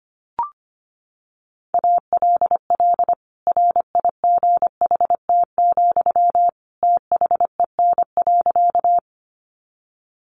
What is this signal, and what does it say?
Morse code, heard as a keyed tone, about 25 wpm, 700 Hz. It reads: ALL RIGHT, THEN.
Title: Sentences from Top 200 Words 25wpm